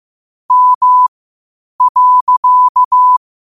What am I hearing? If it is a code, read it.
Morse code, heard as a keyed tone, about 15 wpm, spaced as for 8 wpm, 1000 Hz. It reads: M.
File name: qotdfiles / 8wpm